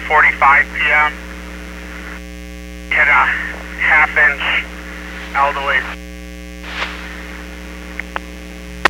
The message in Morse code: none

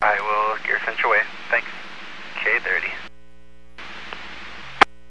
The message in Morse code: none

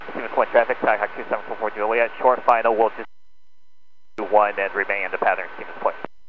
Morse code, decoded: none

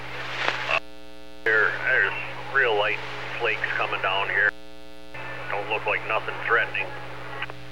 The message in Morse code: none

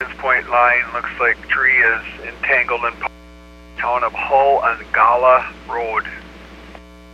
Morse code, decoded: none

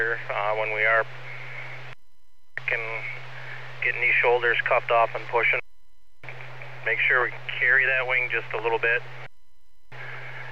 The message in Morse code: none

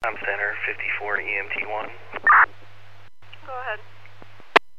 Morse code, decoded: none

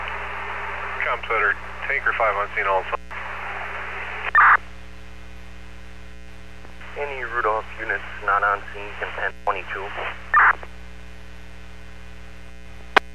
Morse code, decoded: none

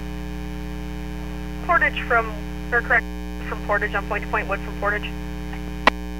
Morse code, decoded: none

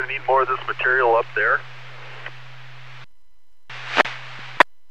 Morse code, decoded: none

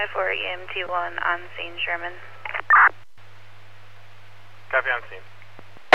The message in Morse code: none

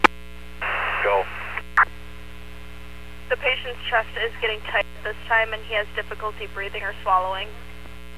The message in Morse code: none

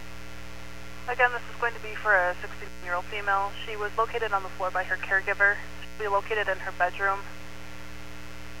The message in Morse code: none

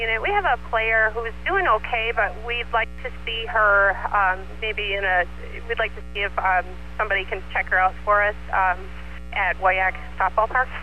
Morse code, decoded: none

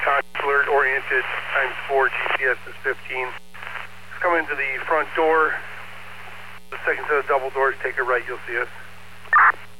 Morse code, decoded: none

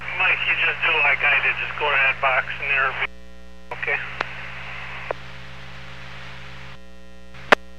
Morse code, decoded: none

